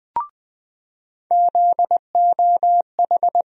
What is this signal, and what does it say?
Morse code, heard as a keyed tone, about 20 wpm, 700 Hz. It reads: ZOH
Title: USA Call Sign Postfixes - 3 Letters 20wpm